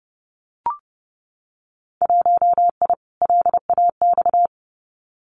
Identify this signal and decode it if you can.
Morse code, heard as a keyed tone, about 30 wpm, 700 Hz. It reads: 1I LAX
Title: Winter Field Day Exchanges 30wpm